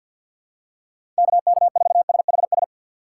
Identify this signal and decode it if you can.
Morse code, heard as a keyed tone, about 47 wpm, 700 Hz. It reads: KK4SHS
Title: USA Call Signs - 6 Characters - 3x Character Spacing - 50wpm